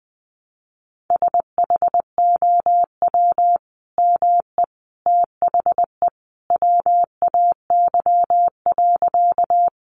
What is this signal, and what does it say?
Morse code, heard as a keyed tone, about 20 wpm, 700 Hz. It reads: SHOW ME THE WAY.
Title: Sentences from Top 200 Words 20wpm